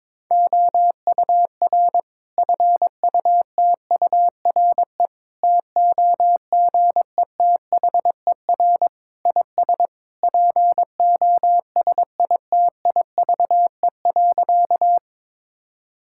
Morse code, heard as a keyed tone, about 21 wpm, 700 Hz. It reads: OUR FUTURE TOGETHER IS POSITIVE.
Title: Sentences from Top 1000 Words - Review Part 5 - 22wpm